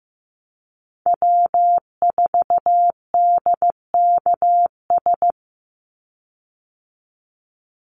Morse code, heard as a keyed tone, about 15 wpm, 700 Hz. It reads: W4DKS
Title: Call Signs Repeated 3x - 15wpm